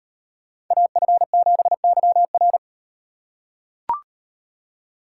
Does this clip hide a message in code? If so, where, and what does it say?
Yes: Morse code from 0.7 to 2.6 s, heard as a keyed tone, about 35 wpm, 700 Hz. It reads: AF7YR